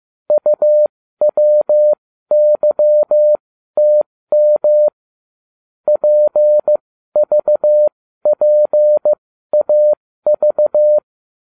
Morse code, heard as a keyed tone, about 15 wpm, spaced as for 12 wpm, 600 Hz. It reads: UWYTM PVPAV